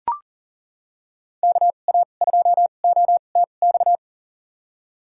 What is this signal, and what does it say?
Morse code, heard as a keyed tone, about 37 wpm, spaced as for 28 wpm, 700 Hz. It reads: KA2OTX
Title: USA Call Signs - 2x Character Spacing - 40wpm